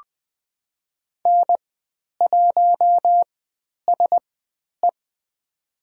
Morse code, heard as a keyed tone, about 20 wpm, spaced as for 10 wpm, 700 Hz. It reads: N1SE
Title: 109 Lesson 18g - Letter m - Call Signs 20wpm